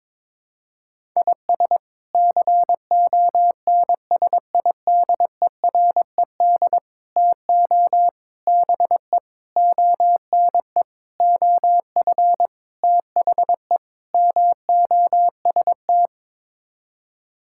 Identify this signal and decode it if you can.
Morse code, heard as a keyed tone, about 22 wpm, 700 Hz. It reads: IS CONSIDERED TO BE ONE OF THE MOST
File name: practice / 22wpm / Sets of 8 Words - 22wpm